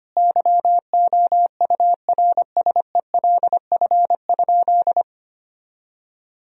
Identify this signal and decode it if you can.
Morse code, heard as a keyed tone, about 25 wpm, 700 Hz. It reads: YOURSELF?
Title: Sentences from Top 1000 Words - Review Part 4 - 25wpm